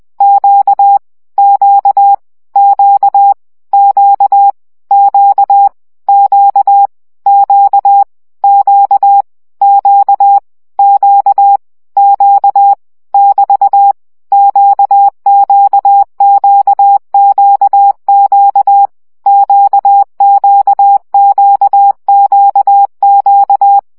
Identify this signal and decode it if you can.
Morse code, heard as a keyed tone, about 20 wpm, 800 Hz. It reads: Q Q Q Q Q Q Q Q Q Q Q <BT> QQQQQ QQQQQ
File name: CW gyakorlas q 17 baud